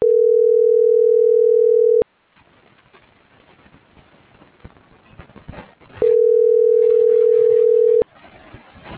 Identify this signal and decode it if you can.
Ringback tone (North America), 440 + 480 Hz.